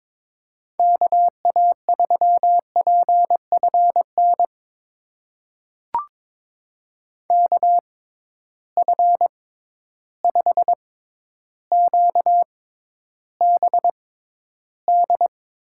Morse code, heard as a keyed tone, about 22 wpm, spaced as for 11 wpm, 700 Hz. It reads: KA3PFN KF5QBD
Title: USA Call Signs - 6 Characters - 6x Character Spacing - 22wpm